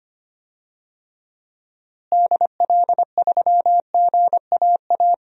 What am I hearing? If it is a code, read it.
Morse code, heard as a keyed tone, about 25 wpm, 700 Hz. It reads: DL3GAA
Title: DXCC-100 Call Signs - 6 Characters - Repeated 3x - 25wpm